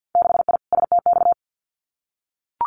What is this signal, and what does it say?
Morse code, heard as a keyed tone, about 50 wpm, 700 Hz. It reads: 6I STX
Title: Winter Field Day Exchanges Encore 50wpm